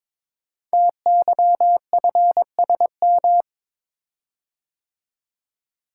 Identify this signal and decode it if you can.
Morse code, heard as a keyed tone, about 22 wpm, 700 Hz. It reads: TYFSM
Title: Code Groups - 5 Letters - 22wpm